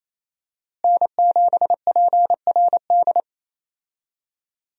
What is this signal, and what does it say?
Morse code, heard as a keyed tone, about 28 wpm, 700 Hz. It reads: N7PRD
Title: USA Call Signs - 5 Characters - Repeated 2x - 28wpm